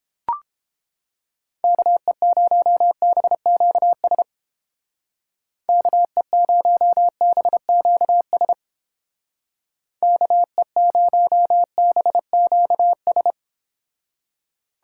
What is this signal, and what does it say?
Morse code, heard as a keyed tone, about 29 wpm, 700 Hz. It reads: KE0BQS KE0BQS KE0BQS